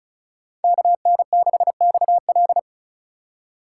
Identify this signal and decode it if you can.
Morse code, heard as a keyed tone, about 34 wpm, 700 Hz. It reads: KN6XL
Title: USA Call Signs - 5 Characters - 2x Character Spacing - 35wpm